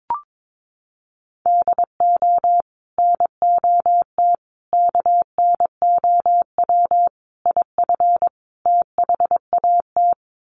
Morse code, heard as a keyed tone, about 22 wpm, 700 Hz. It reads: DO NOT KNOW IF THAT
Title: Sets of 5 Words Encore2 22wpm